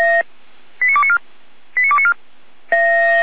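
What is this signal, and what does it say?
Five-tone selective call (ZVEI), heard as quick runs of five tones, tones of about 70 ms. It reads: 89294 89294
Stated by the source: ZVEI five-tone selective call 89229, received by SDR.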